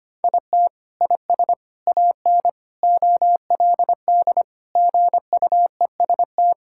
Morse code, heard as a keyed tone, about 25 wpm, 700 Hz. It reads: IT IS AN OLD GUEST
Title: Sentences from Top 1600 Words 25wpm